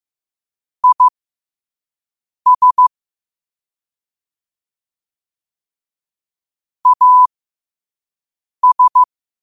Morse code, heard as a keyed tone, about 15 wpm, spaced as for 5 wpm, 1000 Hz. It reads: IS AS
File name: qotdfiles / 5wpm